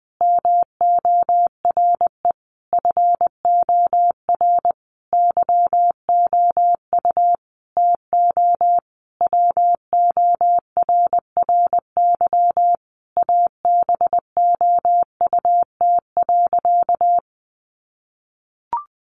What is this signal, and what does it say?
Morse code, heard as a keyed tone, about 20 wpm, 700 Hz. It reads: MORE FOR YOU TO WORRY ABOUT.
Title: Sentences from Top 1100 Words 20wpm